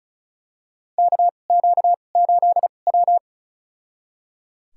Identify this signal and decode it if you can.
Morse code, heard as a keyed tone, about 34 wpm, spaced as for 25 wpm, 700 Hz. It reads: KQ8W
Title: USA Call Signs - 4 Characters - 2x Character Spacing - 35wpm